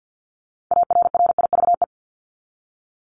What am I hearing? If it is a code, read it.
Morse code, heard as a keyed tone, about 50 wpm, 700 Hz. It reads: ARRIVE
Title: ED Suffix - Top 100 Words with Base Words 50wpm